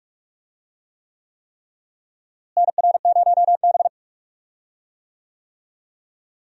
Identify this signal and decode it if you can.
Morse code, heard as a keyed tone, about 46 wpm, 700 Hz. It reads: NR0B
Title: USA Call Signs - 4 Characters - Repeated 3x - 45wpm